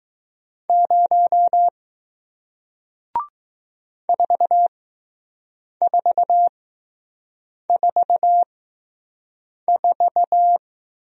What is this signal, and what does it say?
Morse code, heard as a keyed tone, about 20 wpm, spaced as for 6 wpm, 700 Hz. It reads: 0 4444